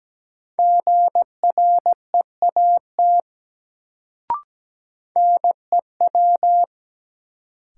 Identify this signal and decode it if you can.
Morse code, heard as a keyed tone, about 17 wpm, 700 Hz. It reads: GREAT NEW
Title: Sets of 2 Words - New Word Drill - 17wpm